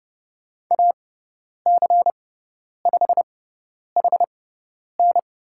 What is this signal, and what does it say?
Morse code, heard as a keyed tone, about 30 wpm, spaced as for 10 wpm, 700 Hz. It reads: AC5HN
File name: Lesson 18g - Letter m - Call Signs 30wpm